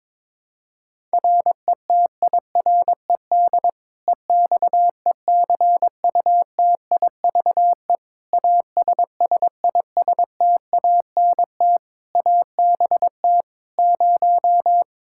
Morse code, heard as a keyed tone, about 22 wpm, 700 Hz. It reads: RETIRED EXECUTIVE ASSISTANT ABT 0
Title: K1USN SST QSO Practice 22wpm